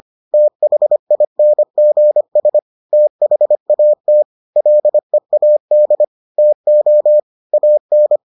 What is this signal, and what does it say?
Morse code, heard as a keyed tone, about 25 wpm, 600 Hz. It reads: THINGS THAT LEAD TO AN